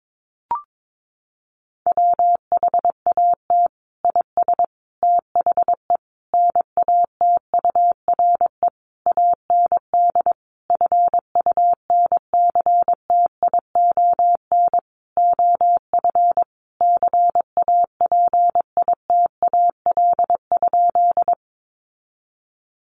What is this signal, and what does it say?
Morse code, heard as a keyed tone, about 22 wpm, 700 Hz. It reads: WHAT IS THE NATURE AND FUNCTION OF CAPITAL?
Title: Sentences from Top 1100 Words 22wpm